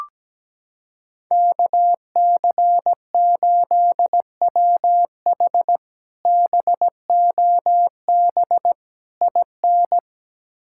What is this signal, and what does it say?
Morse code, heard as a keyed tone, about 17 wpm, 700 Hz. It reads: KC8WH BOB IN
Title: K1USN SST QSO Practice - 17wpm